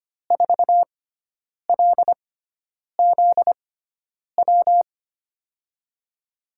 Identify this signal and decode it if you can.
Morse code, heard as a keyed tone, about 25 wpm, spaced as for 9 wpm, 700 Hz. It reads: 4LZW
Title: USA Call Signs - 6 Characters - 6x Character Spacing - 25wpm